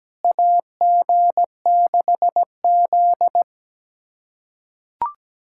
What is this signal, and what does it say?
Morse code, heard as a keyed tone, about 17 wpm, 700 Hz. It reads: AG6Z
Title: USA Callsign DE Callsign - 4 Characters - 17wpm